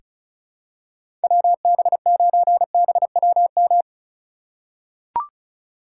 Morse code, heard as a keyed tone, about 36 wpm, 700 Hz. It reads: WB9BWM
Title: USA Call Signs - 6 Characters - 2x Character Spacing - 35wpm